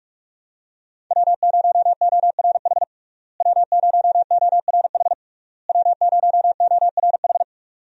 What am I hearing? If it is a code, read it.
Morse code, heard as a keyed tone, about 42 wpm, 700 Hz. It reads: W0ORH W0ORH W0ORH
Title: USA Call Signs - 5 Characters - Repeated 3x - 45wpm